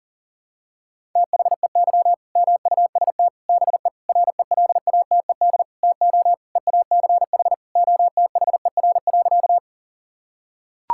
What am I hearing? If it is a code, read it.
Morse code, heard as a keyed tone, about 40 wpm, 700 Hz. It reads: THEY MUST BE RELATED TO EACH OTHER.
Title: Sentences from Top 1000 Words - Review Part 5 - 40wpm